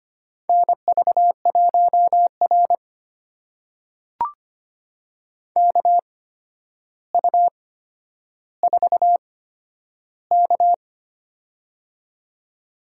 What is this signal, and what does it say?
Morse code, heard as a keyed tone, about 25 wpm, spaced as for 11 wpm, 700 Hz. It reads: NV1R KU4K
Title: USA Call Signs - 4 Characters - 8x Character Spacing - 25wpm